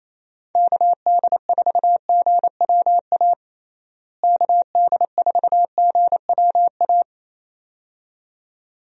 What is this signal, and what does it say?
Morse code, heard as a keyed tone, about 27 wpm, 700 Hz. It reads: KD4GWA KD4GWA
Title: USA Call Signs - 6 Characters - Repeated 2x - 28wpm